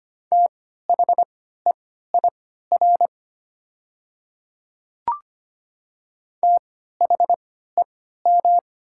Morse code, heard as a keyed tone, about 25 wpm, spaced as for 14 wpm, 700 Hz. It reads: THEIR THEM
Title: 3x Character Spacing - Top 200 Words - 25wpm